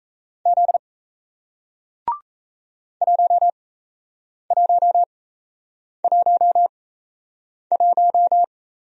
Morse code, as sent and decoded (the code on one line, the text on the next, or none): --.. / .---- .---- .---- .----
Z 1111